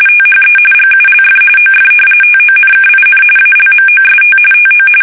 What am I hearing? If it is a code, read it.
RTTY, about 45.45 baud. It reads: THE GULF OF TONKIN ALMOST TO
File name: fsk850 / battleshipnj